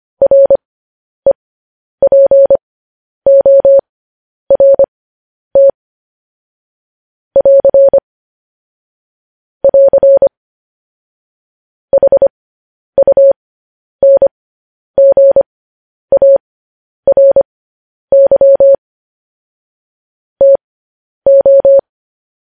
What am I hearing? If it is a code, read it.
Morse code, heard as a keyed tone, about 25 wpm, spaced as for 10 wpm, 550 Hz. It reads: REPORT <AR> <AR> HUNGARY TO
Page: News Headlines in Morse Code at 10 WPM